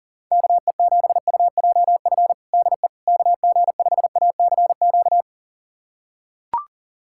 Morse code, heard as a keyed tone, about 40 wpm, 700 Hz. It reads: KE7UJF DE KG5ACQ
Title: USA Callsign DE Callsign - 6 Characters - 40wpm